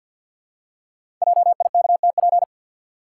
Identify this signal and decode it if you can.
Morse code, heard as a keyed tone, about 46 wpm, 700 Hz. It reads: JIKTP
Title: Code Groups - 5 Letters - 50wpm